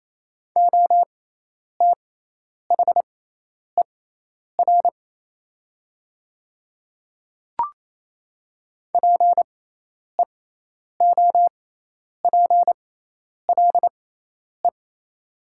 Morse code, heard as a keyed tone, about 28 wpm, spaced as for 10 wpm, 690 Hz. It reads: OTHER PEOPLE
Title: 6x Character Spacing - Top 100 Words - 28wpm